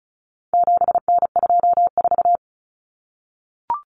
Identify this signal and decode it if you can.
Morse code, heard as a keyed tone, about 34 wpm, 700 Hz. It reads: 7N24
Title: Common Cut Numbers - 4-Digit 35wpm